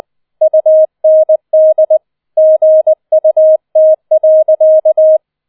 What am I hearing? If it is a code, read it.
Morse code, heard as a keyed tone, about 20 wpm, 630 Hz. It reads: UND GUT.